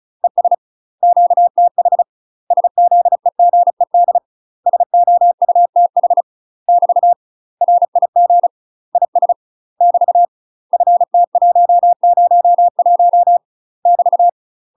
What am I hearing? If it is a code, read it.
Morse code, heard as a keyed tone, about 36 wpm, 700 Hz. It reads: ES QTH SZEGED SOUTH <BT> RIG IS <BT> FT101 <BT>
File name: UK MASTER QSOs TRAINING / 35WPM / UK MASTER QSOs 0033